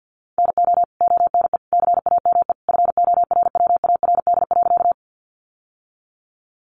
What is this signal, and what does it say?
Morse code, heard as a keyed tone, about 50 wpm, 700 Hz. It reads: NO ONE CAME FORWARD.
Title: Sentences from Top 1300 Words 50wpm